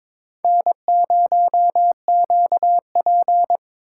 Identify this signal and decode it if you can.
Morse code, heard as a keyed tone, about 23 wpm, 700 Hz. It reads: N0QP
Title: USA Call Signs - 4 Characters - 4x Character Spacing - 22wpm